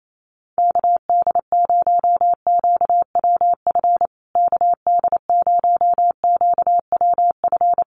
Morse code, heard as a keyed tone, about 28 wpm, 700 Hz. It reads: KD0QWF KD0QWF
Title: USA Call Signs Repeated 3x 28wpm